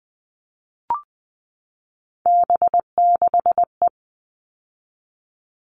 Morse code, heard as a keyed tone, about 20 wpm, 700 Hz. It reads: B6E
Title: All Cut Numbers - 3-Digit 20wpm